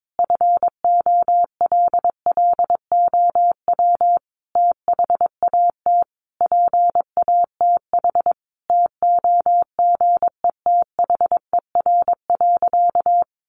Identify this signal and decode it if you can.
Morse code, heard as a keyed tone, about 22 wpm, 700 Hz. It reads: FOLLOW THAT PATH TOGETHER.
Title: Sentences from Top 1400 Words 22wpm